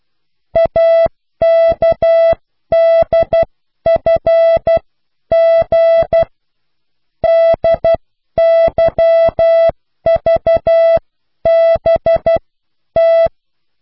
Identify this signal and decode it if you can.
Morse code, heard as a keyed tone, about 12 wpm, spaced as for 10 wpm, 660 Hz. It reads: AKDFG DYVBT